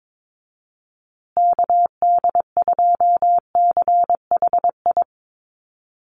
Morse code, heard as a keyed tone, about 22 wpm, 700 Hz. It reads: KD2CHI
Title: USA Call Signs Repeated 2x 22wpm